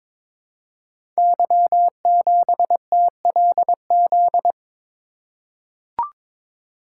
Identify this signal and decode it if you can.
Morse code, heard as a keyed tone, about 22 wpm, 700 Hz. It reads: Y7TLZ
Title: Code Groups - 5 Letter:Numbers - 22wpm